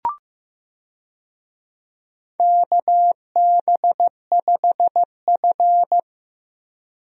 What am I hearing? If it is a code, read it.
Morse code, heard as a keyed tone, about 15 wpm, 700 Hz. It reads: KB5F
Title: USA Call Signs - 4 Characters - Repeated 3x - 15wpm